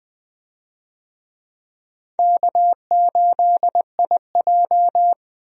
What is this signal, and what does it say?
Morse code, heard as a keyed tone, about 21 wpm, 700 Hz. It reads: K8IJ
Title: USA Call Signs - 4 Characters - Repeated 3x - 20wpm